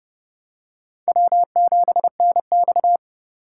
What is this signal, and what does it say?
Morse code, heard as a keyed tone, about 29 wpm, 690 Hz. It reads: W7NX